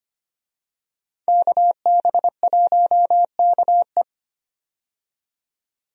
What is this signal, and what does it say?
Morse code, heard as a keyed tone, about 25 wpm, 700 Hz. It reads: KB1KE
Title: USA Call Signs - 5 Characters - 25wpm